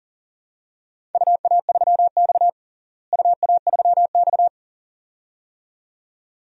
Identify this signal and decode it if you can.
Morse code, heard as a keyed tone, about 40 wpm, 700 Hz. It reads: UA3X UA3X
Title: DXCC-100 Call Signs - 4 Characters - Repeated 2x - 40wpm